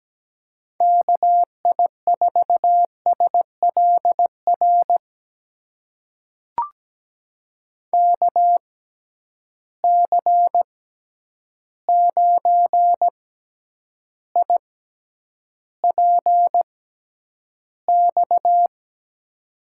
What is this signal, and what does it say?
Morse code, heard as a keyed tone, about 17 wpm, spaced as for 9 wpm, 700 Hz. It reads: KI4SLR KC9IPX
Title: USA Call Signs - 6 Characters - 6x Character Spacing - 17wpm